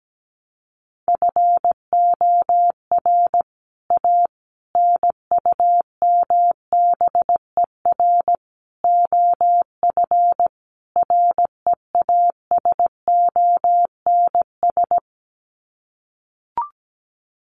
Morse code, heard as a keyed tone, about 17 wpm, 700 Hz. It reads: FOR A NUMBER OF REASONS
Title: Sets of 5 Words Encore2 17wpm